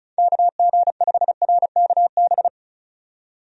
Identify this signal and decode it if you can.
Morse code, heard as a keyed tone, about 34 wpm, 700 Hz. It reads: KG5RKB
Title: USA Call Signs - 6 Characters - 35wpm